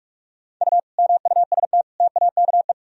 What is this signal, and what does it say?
Morse code, heard as a keyed tone, about 46 wpm, 700 Hz. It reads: U MUST TAKE
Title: Sentences from Top 200 Words and Abbreviations 45wpm